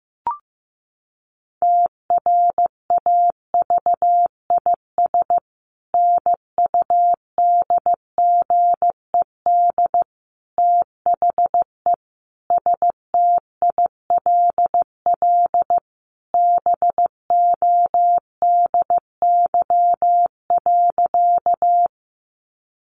Morse code, heard as a keyed tone, about 14 wpm, 700 Hz. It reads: TRAVIS NUDGED THE STILL BODY.
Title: A Sound Of Thunder 15wpm